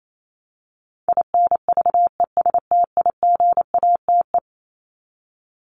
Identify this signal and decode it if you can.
Morse code, heard as a keyed tone, about 28 wpm, 700 Hz. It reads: INVESTIGATE
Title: Top 1751-2000 Words 28wpm